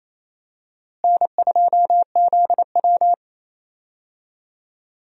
Morse code, heard as a keyed tone, about 28 wpm, 700 Hz. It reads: N2ZW
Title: USA Call Signs - 4 Characters - Repeated 2x - 28wpm